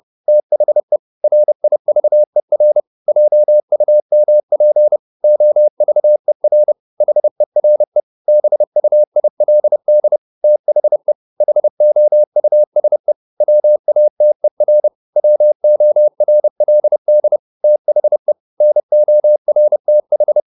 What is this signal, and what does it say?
Morse code, heard as a keyed tone, about 30 wpm, 600 Hz. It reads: THE RIVER JUMP OVER HERE BUILD THE HOUSE WATER WORLD THE NORTH